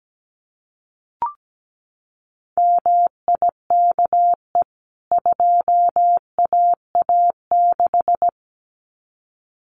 Morse code, heard as a keyed tone, about 17 wpm, 700 Hz. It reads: MIKE 2AA6
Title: CWT Exchange 17wpm